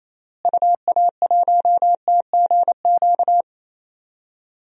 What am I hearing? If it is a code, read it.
Morse code, heard as a keyed tone, about 28 wpm, 700 Hz. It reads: UA1TGQ
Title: DXCC-100 Call Signs - 6 Characters - Repeated 2x - 28wpm